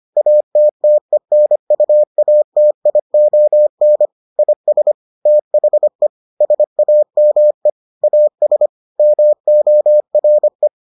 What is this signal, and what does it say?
Morse code, heard as a keyed tone, about 25 wpm, 600 Hz. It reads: ATTENUATION IS THE SAME AS MORE